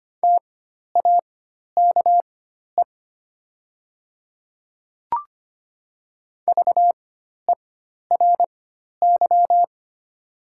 Morse code, heard as a keyed tone, about 25 wpm, spaced as for 12 wpm, 700 Hz. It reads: TAKE VERY
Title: 4x Character Spacing - Top 200 Words - 25wpm